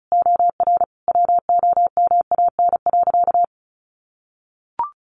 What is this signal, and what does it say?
Morse code, heard as a keyed tone, about 34 wpm, 700 Hz. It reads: OR WOMAN.
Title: Sentences from Top 1400 Words 35wpm